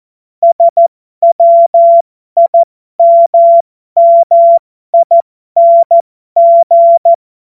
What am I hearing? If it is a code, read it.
Morse code, heard as a keyed tone, about 14 wpm, spaced as for 12 wpm, 680 Hz. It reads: SWIMMING
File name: Morsekurs / Witze